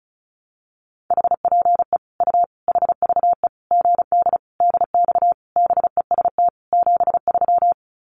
Morse code, heard as a keyed tone, about 34 wpm, 700 Hz. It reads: HPE U HVE GD DX BEST 73
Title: Common QSO Phrases 35wpm